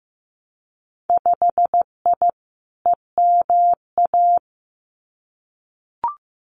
Morse code, heard as a keyed tone, about 15 wpm, 700 Hz. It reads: 5I EMA
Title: Winter Field Day Exchanges Encore 15wpm